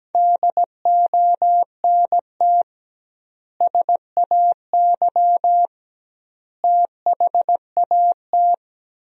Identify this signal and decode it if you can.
Morse code, heard as a keyed tone, about 17 wpm, 700 Hz. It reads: DONT SAY THAT
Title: 2x Word-Spacing - Sentences from Top 300 Words - 17wpm